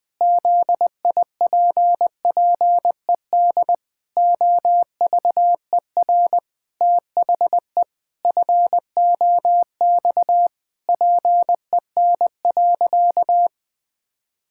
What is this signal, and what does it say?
Morse code, heard as a keyed tone, about 20 wpm, 700 Hz. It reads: ZIPPED OVER THE FOX PEN.
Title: Pangram Challenge v2 - 20wpm